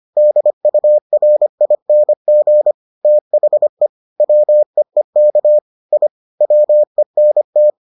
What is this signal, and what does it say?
Morse code, heard as a keyed tone, about 25 wpm, 600 Hz. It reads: DURING THE WEEK I WENT